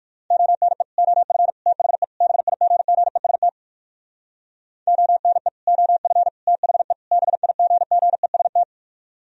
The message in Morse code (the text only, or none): ONE OF THE BIGGEST ONE OF THE BIGGEST